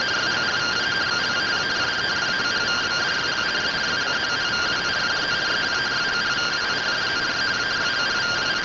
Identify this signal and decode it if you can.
Two-tone teleprinter signal, about 170 Hz shift.